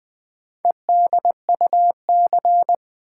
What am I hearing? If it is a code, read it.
Morse code, heard as a keyed tone, about 20 wpm, 700 Hz. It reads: EDUC
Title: Build A Word - 2 Letters At A Time - 8-Letter Words - 20wpm